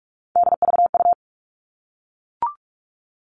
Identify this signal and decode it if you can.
Morse code, heard as a keyed tone, about 44 wpm, 700 Hz. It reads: DVU
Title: All Cut Numbers - 3-Digit 45wpm